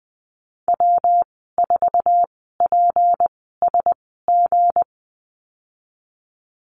Morse code, heard as a keyed tone, about 20 wpm, spaced as for 14 wpm, 700 Hz. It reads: W4PSG
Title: 161 Lesson 26g - Letter g - Call Signs 20wpm